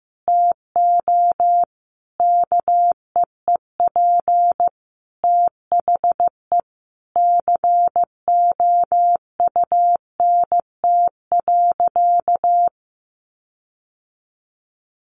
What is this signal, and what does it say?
Morse code, heard as a keyed tone, about 15 wpm, 700 Hz. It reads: TO KEEP THE COUNT.
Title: Sentences from Top 1500 Words 15wpm